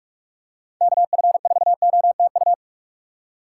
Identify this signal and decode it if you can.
Morse code, heard as a keyed tone, about 46 wpm, 700 Hz. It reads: KF4OTU
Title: USA Call Signs - 6 Characters - 2x Character Spacing - 45wpm